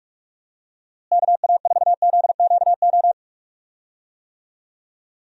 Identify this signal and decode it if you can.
Morse code, heard as a keyed tone, about 45 wpm, 700 Hz. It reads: KA4ZQO
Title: USA Call Signs - 6 Characters - Repeated 3x - 45wpm